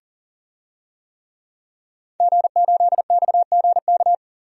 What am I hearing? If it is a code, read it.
Morse code, heard as a keyed tone, about 40 wpm, 700 Hz. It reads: G8XGK
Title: DXCC-100 Call Signs - 5 Characters - 40wpm